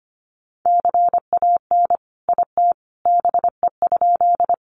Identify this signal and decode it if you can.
Morse code, heard as a keyed tone, about 25 wpm, 700 Hz. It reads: CAN IT BE?
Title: Sentences from Top 100 Words 25wpm